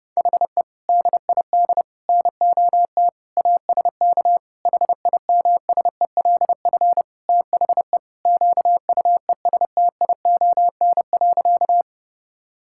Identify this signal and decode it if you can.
Morse code, heard as a keyed tone, about 30 wpm, 700 Hz. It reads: HE DID NOT ASK HIMSELF THE QUESTION.
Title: Sentences from Top 1600 Words 30wpm